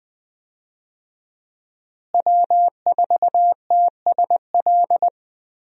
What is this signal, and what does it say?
Morse code, heard as a keyed tone, about 20 wpm, 700 Hz. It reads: W4TSL
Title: USA Call Signs - 5 Characters - Repeated 2x - 20wpm